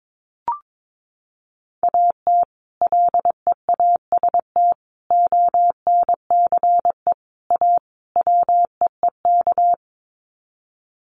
Morse code, heard as a keyed tone, about 22 wpm, 700 Hz. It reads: AT LEAST ONCE A WEEK
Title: Sets of 5 Words Encore2 22wpm